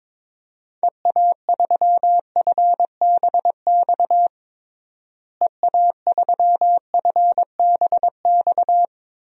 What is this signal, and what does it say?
Morse code, heard as a keyed tone, about 22 wpm, 700 Hz. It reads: EA3FBX EA3FBX
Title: DXCC-100 Call Signs - 6 Characters - Repeated 2x - 22wpm